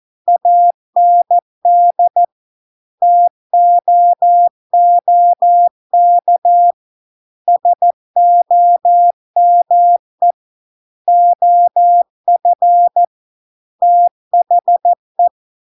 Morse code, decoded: AND TOOK SOME OF THE

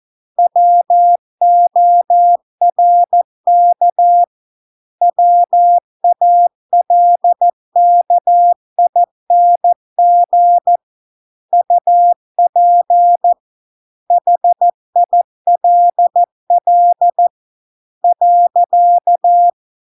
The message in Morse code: WORK WALKING UP HILL .

